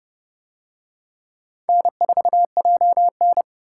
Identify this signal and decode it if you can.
Morse code, heard as a keyed tone, about 29 wpm, 700 Hz. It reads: N4JN